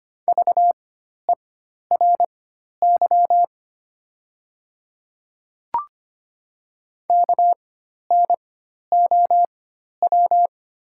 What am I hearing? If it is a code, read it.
Morse code, heard as a keyed tone, about 25 wpm, spaced as for 12 wpm, 700 Hz. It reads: VERY KNOW
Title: 4x Character Spacing - Top 300 Words - 25wpm